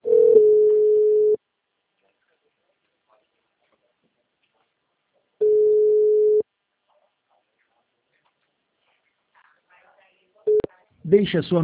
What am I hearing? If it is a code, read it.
Ringback tone (Europe, 425 Hz), with speech.